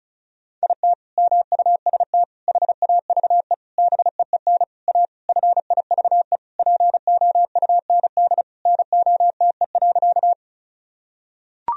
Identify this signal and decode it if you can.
Morse code, heard as a keyed tone, about 34 wpm, 700 Hz. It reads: IT MUST HAVE BEEN A FIVE POUND NOTE.
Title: Sentences from Top 1000 Words - Review Part 2 - 35wpm